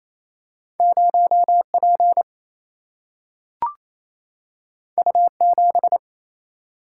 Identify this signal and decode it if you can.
Morse code, heard as a keyed tone, about 28 wpm, 700 Hz. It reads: 0 P U 7
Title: Single-Letter Single-Number 28wpm